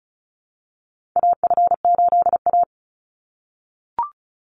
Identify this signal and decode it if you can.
Morse code, heard as a keyed tone, about 36 wpm, 700 Hz. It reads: AF8A
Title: Call Signs - 35wpm